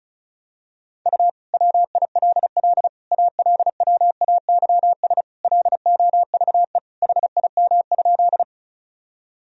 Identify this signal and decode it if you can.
Morse code, heard as a keyed tone, about 36 wpm, 700 Hz. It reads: U WILL ALWAYS LOVE HIM?
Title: Sentences from Top 400 Words and Abbreviations 35wpm